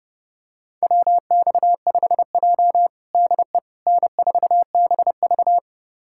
Morse code, heard as a keyed tone, about 30 wpm, 700 Hz. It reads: WX5J DE N4BV
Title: USA Callsign DE Callsign - 4 Characters - 30wpm